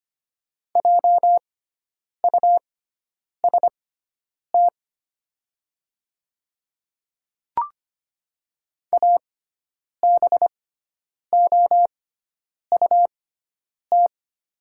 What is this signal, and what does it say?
Morse code, heard as a keyed tone, about 25 wpm, spaced as for 9 wpm, 700 Hz. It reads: JUST ABOUT
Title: 6x Character Spacing - Top 300 Words - 25wpm